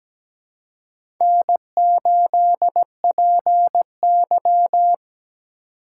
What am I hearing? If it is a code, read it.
Morse code, heard as a keyed tone, about 17 wpm, 700 Hz. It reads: N8PY
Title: USA Call Signs - 4 Characters - 3x Character Spacing - 17wpm